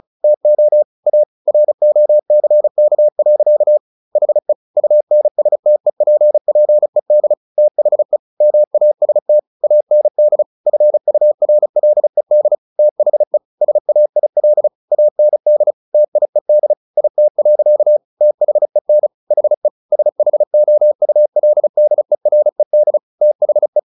Morse code, heard as a keyed tone, about 34 wpm, 600 Hz. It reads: TO A ROCK. HE UNSTEPPED THE MAST AND FURLED THE SAIL AND TIED IT. THEN HE SHOULDERED THE
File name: the old man and the sea 2 35wpm